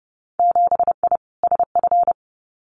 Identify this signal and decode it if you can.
Morse code, heard as a keyed tone, about 30 wpm, 700 Hz. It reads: 7I SF
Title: Winter Field Day Exchanges Encore 30wpm